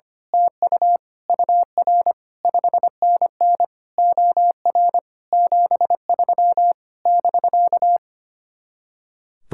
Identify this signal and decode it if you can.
Morse code, heard as a keyed tone, about 24 wpm, 700 Hz. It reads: TU UR 5NN OR 73 <BK>
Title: Speed-Racing Common QSO Phrases Encore - 25wpm